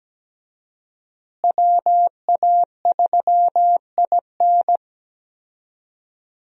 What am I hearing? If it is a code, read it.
Morse code, heard as a keyed tone, about 17 wpm, 700 Hz. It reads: WA3IN
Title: USA Call Signs - 5 Characters - Repeated 2x - 17wpm